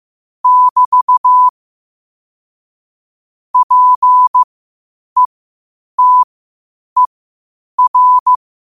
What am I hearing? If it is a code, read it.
Morse code, heard as a keyed tone, about 15 wpm, spaced as for 8 wpm, 1000 Hz. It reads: <BT> PETER
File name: qotdfiles / 8wpm